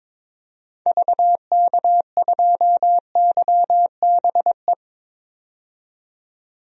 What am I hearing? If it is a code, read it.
Morse code, heard as a keyed tone, about 22 wpm, 700 Hz. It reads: VK2YBE
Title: DXCC-100 Call Signs - 6 Characters - 22wpm